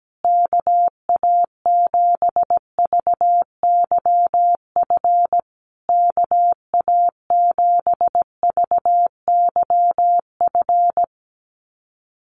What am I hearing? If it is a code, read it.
Morse code, heard as a keyed tone, about 17 wpm, 700 Hz. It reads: KA7VYF KA7VYF
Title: USA Call Signs Repeated 3x 17wpm